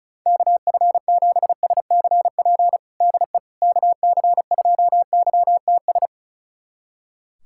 Morse code, heard as a keyed tone, about 34 wpm, 700 Hz. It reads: KF7SCP DE KC2YTS
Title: USA Callsign DE Callsign - 6 Characters - 35wpm